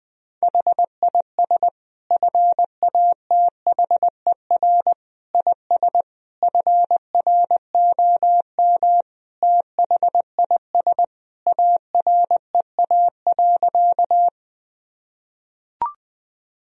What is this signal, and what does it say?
Morse code, heard as a keyed tone, about 21 wpm, 700 Hz. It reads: HIS FATHER IS FROM THIS AREA.
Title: Sentences from Top 1000 Words - Review Part 2 - 20wpm